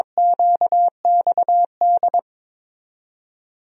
Morse code, heard as a keyed tone, about 21 wpm, 700 Hz. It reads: QXD
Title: USA Call Signs - 5 Characters - 22wpm